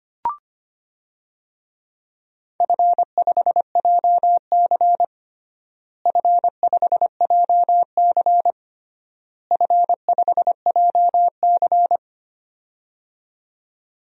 Morse code, heard as a keyed tone, about 25 wpm, 700 Hz. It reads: F5JC F5JC F5JC
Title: DXCC-100 Call Signs - 4 Characters - Repeated 3x - 25wpm